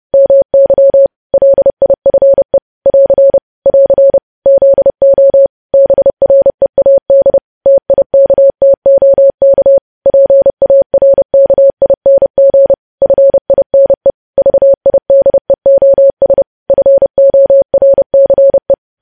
Morse code, heard as a keyed tone, about 30 wpm, 550 Hz. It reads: MY LIFE <AR> <AR> ZO BREAD TIKTOK PARKING FINE VIDEOS FORCE
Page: news headlines in morse code at 30 wpm